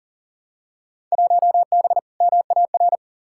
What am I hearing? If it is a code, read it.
Morse code, heard as a keyed tone, about 40 wpm, 700 Hz. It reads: 1B MAR
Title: Field Day Exchanges Encore - 40wpm